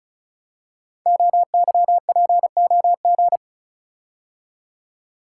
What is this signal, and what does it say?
Morse code, heard as a keyed tone, about 36 wpm, 700 Hz. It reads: OYPOG
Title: Code Groups - 5 Letters - 35wpm